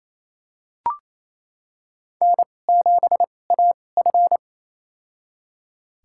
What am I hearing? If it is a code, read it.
Morse code, heard as a keyed tone, about 28 wpm, spaced as for 20 wpm, 700 Hz. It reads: N7AF
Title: USA Call Signs - 4 Characters - 2x Character Spacing - 28wpm